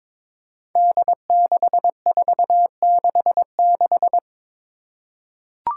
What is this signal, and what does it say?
Morse code, heard as a keyed tone, about 22 wpm, 700 Hz. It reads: D6466
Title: Five Challenging Letter-Number Combinations 22wpm